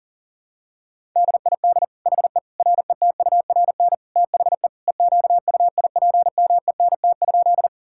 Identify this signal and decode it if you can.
Morse code, heard as a keyed tone, about 40 wpm, 700 Hz. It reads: DID HE RETURN THE EQUIPMENT?
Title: Sentences from Top 1000 Words - Review Part 5 - 40wpm